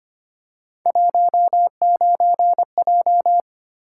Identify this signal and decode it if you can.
Morse code, heard as a keyed tone, about 25 wpm, 700 Hz. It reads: 19J
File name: Three Letter-Number Combinations - 25wpm